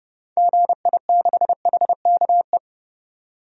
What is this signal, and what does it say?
Morse code, heard as a keyed tone, about 29 wpm, 700 Hz. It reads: GI6HKE
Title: DXCC-100 Call Signs - 6 Characters - 30wpm